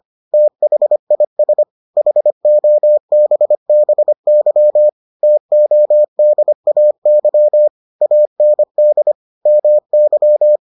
Morse code, heard as a keyed tone, about 25 wpm, 600 Hz. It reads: THIS HOBBY TODAY AND MY